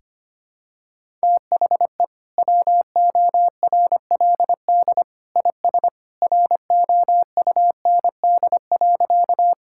Morse code, heard as a keyed tone, about 25 wpm, 700 Hz. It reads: THE WORLD IS ROUND.
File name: practice / 25wpm / Sentences with 2-Word Phrases - 25wpm